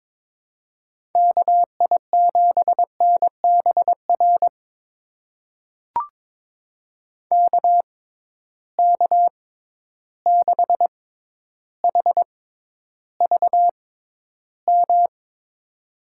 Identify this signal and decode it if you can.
Morse code, heard as a keyed tone, about 22 wpm, spaced as for 11 wpm, 700 Hz. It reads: KI7NBR KK6HVM
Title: USA Call Signs - 6 Characters - 6x Character Spacing - 22wpm